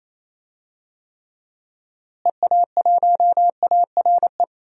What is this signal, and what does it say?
Morse code, heard as a keyed tone, about 28 wpm, 700 Hz. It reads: EA1ARE
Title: DXCC-100 Call Signs - 6 Characters - Repeated 2x - 28wpm